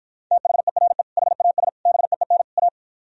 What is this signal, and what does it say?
Morse code, heard as a keyed tone, about 54 wpm, 700 Hz. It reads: THERE HAS BEEN A